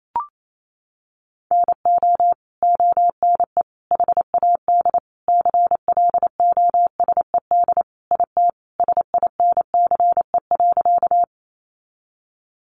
Morse code, heard as a keyed tone, about 28 wpm, 700 Hz. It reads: NO ONE HAD CLOSED IT SINCE.
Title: Sentences from Top 1400 Words 28wpm